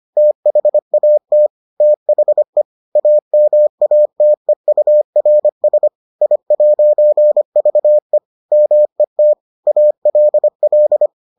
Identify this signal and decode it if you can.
Morse code, heard as a keyed tone, about 25 wpm, 600 Hz. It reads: THAT THE AMATEURS I'VE MET ALL